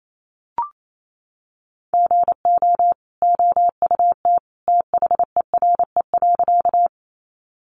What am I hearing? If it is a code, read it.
Morse code, heard as a keyed tone, about 28 wpm, 700 Hz. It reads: GO OUT THERE.
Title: A Sound Of Thunder 28wpm